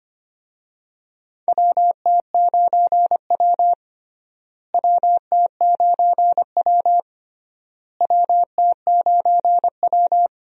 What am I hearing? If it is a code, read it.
Morse code, heard as a keyed tone, about 25 wpm, 700 Hz. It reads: WT9W WT9W WT9W
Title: USA Call Signs - 4 Characters - Repeated 3x - 25wpm